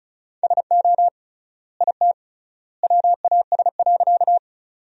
Morse code, heard as a keyed tone, about 34 wpm, 700 Hz. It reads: SO IT WAS.
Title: 3x Word-Spacing - Sentences from Top 100 Words - 35wpm